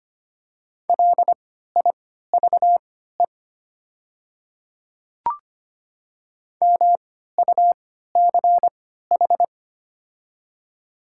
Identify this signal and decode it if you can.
Morse code, heard as a keyed tone, about 25 wpm, spaced as for 14 wpm, 700 Hz. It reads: LIVE MUCH
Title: 3x Character Spacing - Top 200 Words - 25wpm